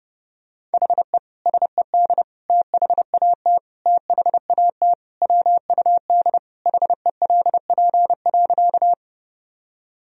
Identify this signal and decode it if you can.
Morse code, heard as a keyed tone, about 32 wpm, 700 Hz. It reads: HE SED THAT THAT WUD HELP.
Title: Sentences from Top 200 Words and Abbreviations 30wpm